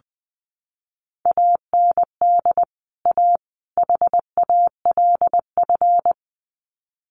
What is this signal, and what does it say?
Morse code, heard as a keyed tone, about 19 wpm, 700 Hz. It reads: AND A HALF
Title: Sets of 3 Words 20wpm